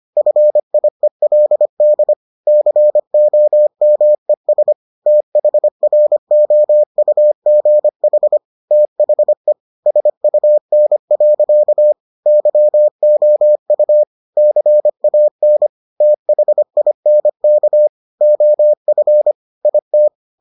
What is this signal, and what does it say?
Morse code, heard as a keyed tone, about 25 wpm, 600 Hz. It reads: FIELD COMES THROUGH THE SUN. YOU CAN THINK OF IT